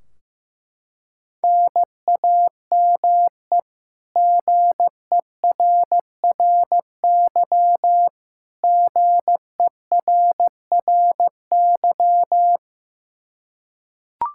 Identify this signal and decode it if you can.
Morse code, heard as a keyed tone, about 15 wpm, 700 Hz. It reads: NAME GERRY GERRY
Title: Practice QSO - Exchange 1 - Beginning - 15wpm